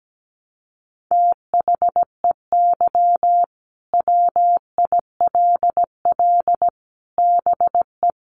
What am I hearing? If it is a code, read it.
Morse code, heard as a keyed tone, about 17 wpm, 700 Hz. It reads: THEY WILL BE
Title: Sets of 5 Words Encore 17wpm